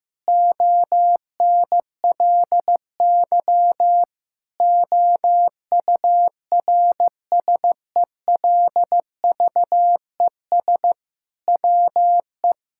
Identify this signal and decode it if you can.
Morse code, heard as a keyed tone, about 15 wpm, 700 Hz. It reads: ONLY OURSELVES WE
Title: Sentences from Top 1600 Words 15wpm